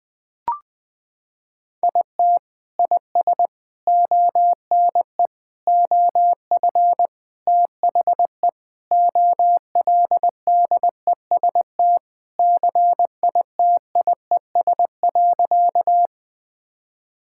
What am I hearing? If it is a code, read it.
Morse code, heard as a keyed tone, about 20 wpm, 700 Hz. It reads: IT IS ONE OF THE OLDEST CITIES.
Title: Sentences with 5-Word Phrases - 20wpm